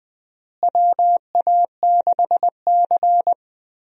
Morse code, heard as a keyed tone, about 21 wpm, 700 Hz. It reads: WA6C
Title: USA Call Signs - 4 Characters - Repeated 3x - 20wpm